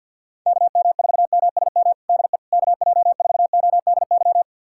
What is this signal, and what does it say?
Morse code, heard as a keyed tone, about 53 wpm, 700 Hz. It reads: KM4MSM DE KJ4ODY
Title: USA Callsign DE Callsign 50wpm